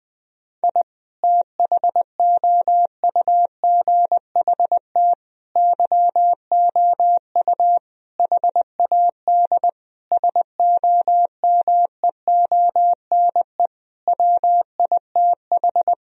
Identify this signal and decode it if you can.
Morse code, heard as a keyed tone, about 20 wpm, 700 Hz. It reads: I THOUGHT YOU HAD SOMEONE WITH
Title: Sentences from Top 1000 Words - Review Part 4 - 20wpm